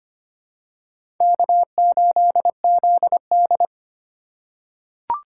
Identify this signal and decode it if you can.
Morse code, heard as a keyed tone, about 25 wpm, 700 Hz. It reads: K8ZD
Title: USA Call Signs - 4x Character Spacing - 25wpm